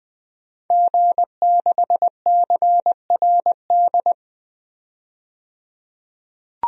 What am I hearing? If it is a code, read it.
Morse code, heard as a keyed tone, about 20 wpm, 700 Hz. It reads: G6CRD
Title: DXCC-100 Call Signs - 5 Characters - 20wpm